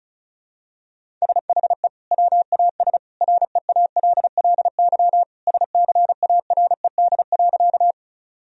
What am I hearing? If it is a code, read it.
Morse code, heard as a keyed tone, about 36 wpm, 700 Hz. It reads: SHE WAS REALLY SCARED.
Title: Sentences from Top 1000 Words - Review Part 4 - 35wpm